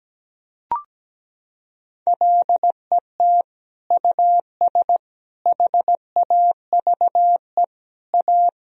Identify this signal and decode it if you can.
Morse code, heard as a keyed tone, about 17 wpm, 700 Hz. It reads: LET US HAVE A
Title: Sentences from Top 1000 Words - Review Part 1 - 17wpm